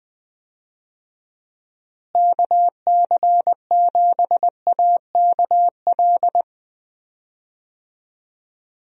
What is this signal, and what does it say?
Morse code, heard as a keyed tone, about 20 wpm, 700 Hz. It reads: KC7AKL
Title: USA Call Signs - 6 Characters - Repeated 2x - 20wpm